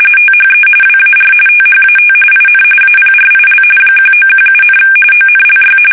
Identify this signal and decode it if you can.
RTTY, about 45.45 baud. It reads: THE WAR THAT IN APRIL 1917, ONLY
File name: fsk850